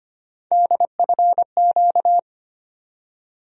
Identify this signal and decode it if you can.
Morse code, heard as a keyed tone, about 25 wpm, 700 Hz. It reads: DFQ
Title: USA Call Sign Postfixes - 3 Letters 25wpm